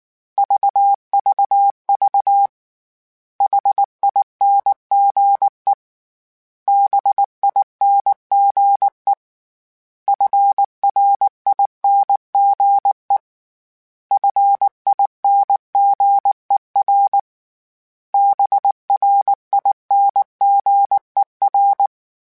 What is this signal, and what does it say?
Morse code, heard as a keyed tone, about 19 wpm, 800 Hz. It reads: VVV HINGE BINGE FRINGE FINGER BRINGER
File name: Special INGs / Special INGs-mp3s / Hidden ings @5